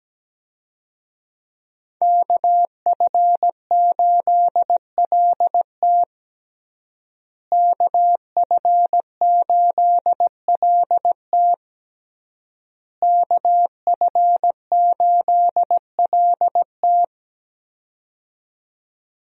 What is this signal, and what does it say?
Morse code, heard as a keyed tone, about 17 wpm, 700 Hz. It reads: KF8LT KF8LT KF8LT
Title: USA Call Signs - 5 Characters - Repeated 3x - 17wpm